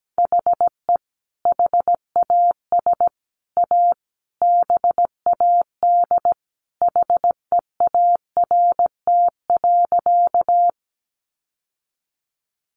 Morse code, heard as a keyed tone, about 17 wpm, 700 Hz. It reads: HE HAS A BAD HEART.
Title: Sentences from Top 1200 Words 17wpm